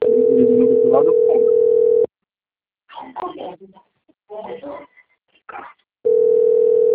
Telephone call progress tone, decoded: ringback tone

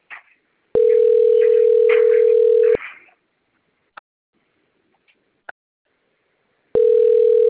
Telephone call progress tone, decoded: ringback tone

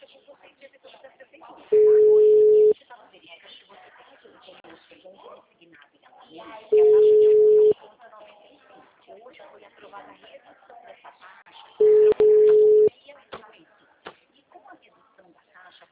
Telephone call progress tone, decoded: ringback tone

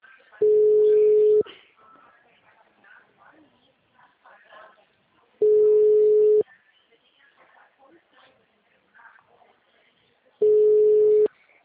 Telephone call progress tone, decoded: ringback tone

